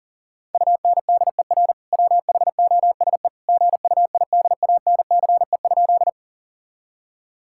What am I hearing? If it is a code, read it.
Morse code, heard as a keyed tone, about 40 wpm, 700 Hz. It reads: UNDER WHOSE GUIDANCE?
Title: Sentences from Top 1600 Words 40wpm